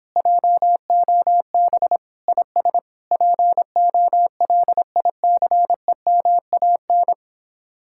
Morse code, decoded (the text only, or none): JOB IS POLICEMAN